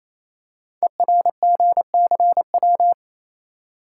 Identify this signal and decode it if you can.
Morse code, heard as a keyed tone, about 28 wpm, 700 Hz. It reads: ERGCW
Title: Code Groups - 5 Letter:Numbers - 28wpm